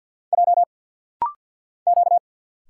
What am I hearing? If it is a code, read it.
Morse code, heard as a keyed tone, about 48 wpm, 690 Hz. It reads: J Q
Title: Single Letter-Number - Rapid-Fire 50wpm